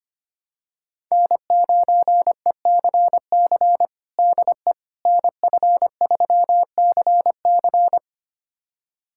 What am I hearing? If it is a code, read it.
Morse code, heard as a keyed tone, about 25 wpm, 700 Hz. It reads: N9ECC DE NF3CC
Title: USA Callsign DE Callsign 25wpm